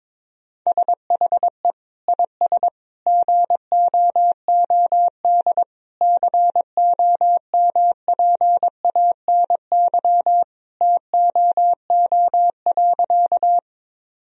Morse code, heard as a keyed tone, about 22 wpm, 700 Hz. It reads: SHE IS GOOD COMPANY TOO.
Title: Sentences from Top 1000 Words - Review Part 3 - 22wpm